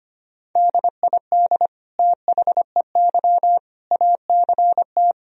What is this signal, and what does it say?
Morse code, heard as a keyed tone, about 25 wpm, 700 Hz. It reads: DID THEY ACT
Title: Sentences from Top 1000 Words - Review Part 4 - 25wpm